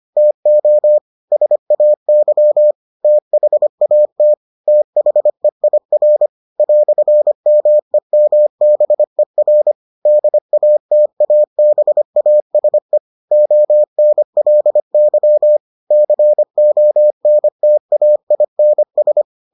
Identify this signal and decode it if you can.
Morse code, heard as a keyed tone, about 25 wpm, 600 Hz. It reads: TO SAY THAT THEIR "MEMBER DATABASE ONLY CONTAINS